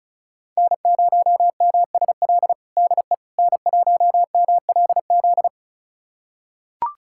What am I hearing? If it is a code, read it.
Morse code, heard as a keyed tone, about 36 wpm, 700 Hz. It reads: N0MSL DE N1MLZ
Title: USA Callsign DE Callsign - 5 Characters - 35wpm